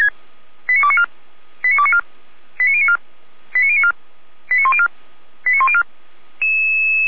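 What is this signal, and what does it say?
Five-tone selective call (ZVEI), heard as quick runs of five tones, tones of about 70 ms. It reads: 89294 89294 89094 89094 89104 89104
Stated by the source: ZVEI five-tone selective call on 455, received by SDR.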